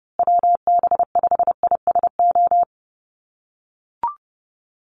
Morse code, heard as a keyed tone, about 30 wpm, 700 Hz. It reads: WB5ISO
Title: USA Call Signs Repeated 3x 30wpm